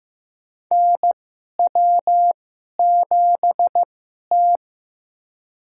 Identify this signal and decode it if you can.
Morse code, heard as a keyed tone, about 15 wpm, spaced as for 11 wpm, 700 Hz. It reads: NW7T
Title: USA Call Signs - 2x Character Spacing - 15wpm